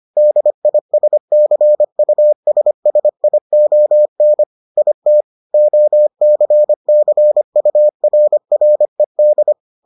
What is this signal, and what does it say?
Morse code, heard as a keyed tone, about 25 wpm, 600 Hz. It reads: DISCUSSION IT OCCURRED